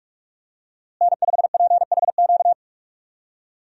Morse code, heard as a keyed tone, about 46 wpm, 700 Hz. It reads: N5PHQ
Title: USA Call Signs - 5 Characters - Repeated 2x - 45wpm